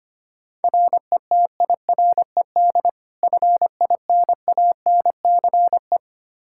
Morse code, heard as a keyed tone, about 25 wpm, 700 Hz. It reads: RETIRED FINANCE
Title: K1USN SST QSO Practice 25wpm